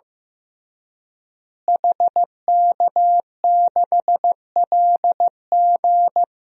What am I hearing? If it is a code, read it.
Morse code, heard as a keyed tone, about 15 wpm, 700 Hz. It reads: HK6LG
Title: DXCC-100 Call Signs - 5 Characters - Repeated 3x - 15wpm